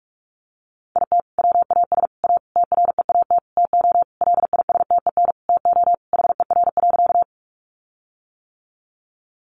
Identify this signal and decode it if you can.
Morse code, heard as a keyed tone, about 46 wpm, 700 Hz. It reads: IT WAS A TREAT TO LISTEN TO HER.
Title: Sentences from Top 1300 Words 45wpm